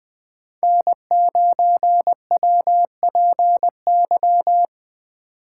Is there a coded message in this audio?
Yes: Morse code from 0.6 to 4.7 s, heard as a keyed tone, about 20 wpm, 700 Hz. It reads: N9WPY